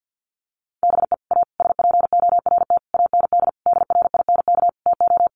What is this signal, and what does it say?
Morse code, heard as a keyed tone, about 50 wpm, 700 Hz. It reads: BE A SPORT AND DRINK TO
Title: Sentences from Top 1500 Words 50wpm